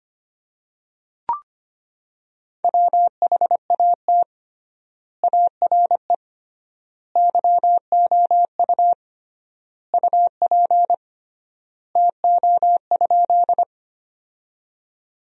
Morse code, heard as a keyed tone, about 25 wpm, 700 Hz. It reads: WHAT ARE YOU UP TO?
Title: 3x Word-Spacing - Sentences from Top 100 Words - 25wpm